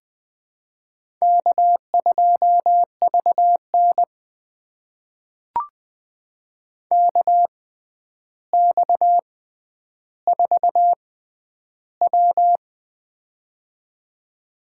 Morse code, heard as a keyed tone, about 20 wpm, spaced as for 10 wpm, 700 Hz. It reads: K2VN KX4W
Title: USA Call Signs - 4 Characters - 6x Character Spacing - 20wpm